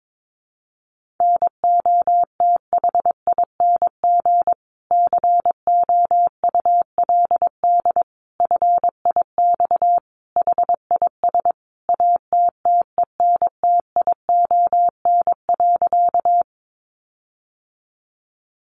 Morse code, heard as a keyed tone, about 22 wpm, 700 Hz. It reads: NOTHING COULD FIX HIS ATTENTION.
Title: Sentences from Top 1400 Words 22wpm